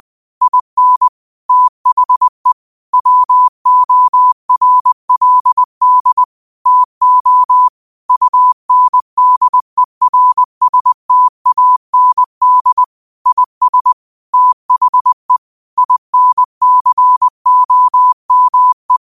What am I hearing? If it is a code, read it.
Morse code, heard as a keyed tone, about 20 wpm, 1000 Hz. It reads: IN THE WORLD TO UNDERSTAND IS THE INCOME
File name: qotdfiles / 20wpm